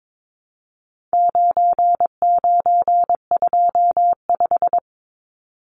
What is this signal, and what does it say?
Morse code, heard as a keyed tone, about 22 wpm, 700 Hz. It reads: 9925
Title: Numbers - 4-Digit 22wpm